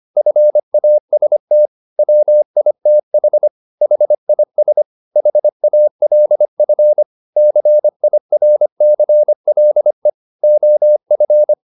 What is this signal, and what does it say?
Morse code, heard as a keyed tone, about 25 wpm, 600 Hz. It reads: FAST WITH HIS HALF CIRCLE OF